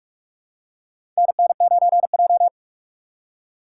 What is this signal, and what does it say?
Morse code, heard as a keyed tone, about 46 wpm, 700 Hz. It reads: NN9J
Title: USA Call Signs - 4 Characters - 2x Character Spacing - 45wpm